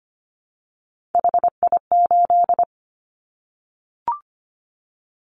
Morse code, heard as a keyed tone, about 25 wpm, 700 Hz. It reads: HI8
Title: Callsign Prefixes for Top 100 DXCC - 25wpm